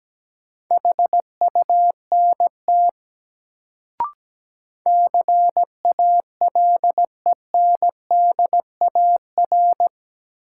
Morse code, heard as a keyed tone, about 17 wpm, 700 Hz. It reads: HUNT CALENDAR
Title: Sentences from Top 1600 Words - New Word Drill - 17wpm